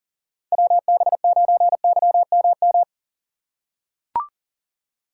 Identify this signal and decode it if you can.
Morse code, heard as a keyed tone, about 40 wpm, 700 Hz. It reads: WB9YMM
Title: USA Call Signs - 6 Characters - 6x Character Spacing - 40wpm